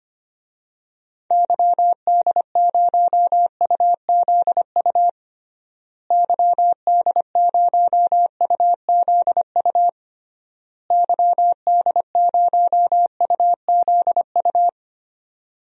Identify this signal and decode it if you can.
Morse code, heard as a keyed tone, about 25 wpm, 700 Hz. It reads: YD0UZU YD0UZU YD0UZU
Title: DXCC-100 Call Signs - 6 Characters - Repeated 3x - 25wpm